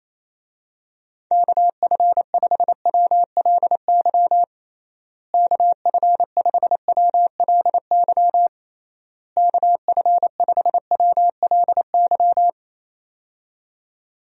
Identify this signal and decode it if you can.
Morse code, heard as a keyed tone, about 28 wpm, 700 Hz. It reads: KF5WLY KF5WLY KF5WLY
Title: USA Call Signs - 6 Characters - Repeated 3x - 28wpm